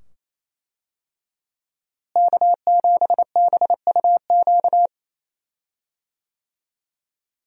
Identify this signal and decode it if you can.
Morse code, heard as a keyed tone, about 27 wpm, 700 Hz. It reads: K7BUQ
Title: USA Call Signs - 5 Characters - Repeated 2x - 28wpm